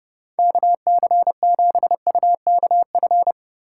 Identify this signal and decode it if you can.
Morse code, heard as a keyed tone, about 30 wpm, 700 Hz. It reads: KC7UKF
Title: USA Call Signs - 6 Characters - 2x Character Spacing - 30wpm